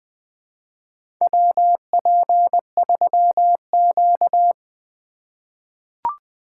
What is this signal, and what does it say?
Morse code, heard as a keyed tone, about 20 wpm, 700 Hz. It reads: WP3Q
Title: USA Call Signs - 4 Characters - 4x Character Spacing - 20wpm